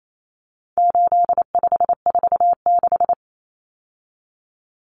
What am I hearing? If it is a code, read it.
Morse code, heard as a keyed tone, about 28 wpm, 700 Hz. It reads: 8546
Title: Common Cut Numbers - 4-Digit 28wpm